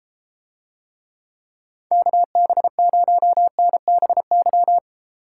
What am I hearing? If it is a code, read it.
Morse code, heard as a keyed tone, about 33 wpm, 700 Hz. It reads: KB0NBY